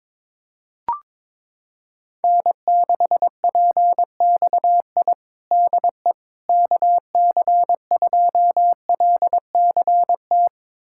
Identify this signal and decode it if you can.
Morse code, heard as a keyed tone, about 22 wpm, 700 Hz. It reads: N6PXI DE KC2LCT
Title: USA Callsign DE Callsign 22wpm